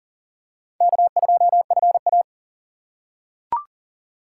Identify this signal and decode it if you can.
Morse code, heard as a keyed tone, about 40 wpm, 700 Hz. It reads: K2FA
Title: USA Call Signs - 4 Characters - 6x Character Spacing - 40wpm